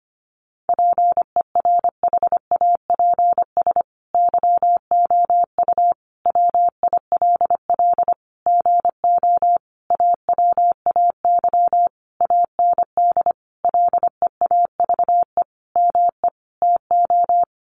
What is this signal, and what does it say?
Morse code, heard as a keyed tone, about 25 wpm, 700 Hz. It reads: PERHAPS YOU WILL GO AWAY AND LEAVE ME TO
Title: Sentences from Top 1300 Words 25wpm